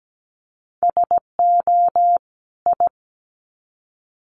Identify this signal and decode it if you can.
Morse code, heard as a keyed tone, about 17 wpm, 700 Hz. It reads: SO I
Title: Sets of 2 Words 17wpm